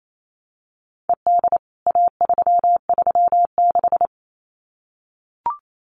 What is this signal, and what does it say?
Morse code, heard as a keyed tone, about 28 wpm, 700 Hz. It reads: ED A336
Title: CWT Exchange 28wpm